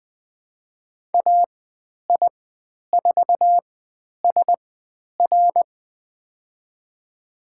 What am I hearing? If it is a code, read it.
Morse code, heard as a keyed tone, about 20 wpm, spaced as for 10 wpm, 700 Hz. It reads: AI4SR
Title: 102 Lesson 17g - Letter u - Call Signs 20wpm